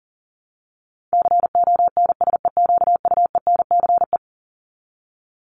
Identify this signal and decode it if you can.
Morse code, heard as a keyed tone, about 37 wpm, 700 Hz. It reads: CONSEQUENCE
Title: LY Suffix - Top 100 Words with Base Words 40wpm